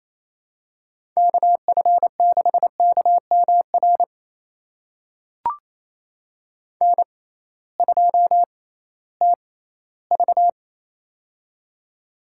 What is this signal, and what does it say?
Morse code, heard as a keyed tone, about 28 wpm, spaced as for 16 wpm, 700 Hz. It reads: KF6KMR N2TV
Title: USA Call Signs - 6x Character Spacing - 28wpm